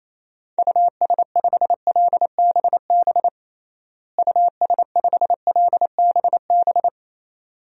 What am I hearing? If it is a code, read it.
Morse code, heard as a keyed tone, about 28 wpm, 700 Hz. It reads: US5LBB US5LBB
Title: DXCC-100 Call Signs - 6 Characters - Repeated 2x - 28wpm